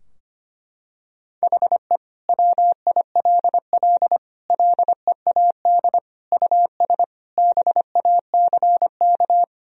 Morse code, heard as a keyed tone, about 25 wpm, 700 Hz. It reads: HE WILL LEAD US BACK
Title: Sentences from Top 1000 Words - Review Part 2 - 25wpm